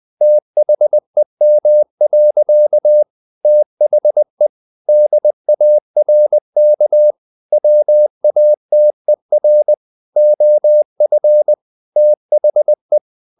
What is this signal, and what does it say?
Morse code, heard as a keyed tone, about 20 wpm, 600 Hz. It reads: THEM. THE DARK WATER OF THE
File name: the old man and the sea part 2 20wpm